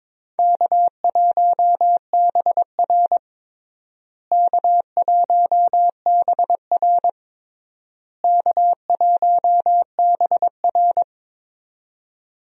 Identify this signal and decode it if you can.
Morse code, heard as a keyed tone, about 22 wpm, 700 Hz. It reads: K1BR K1BR K1BR
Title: USA Call Signs - 4 Characters - Repeated 3x - 22wpm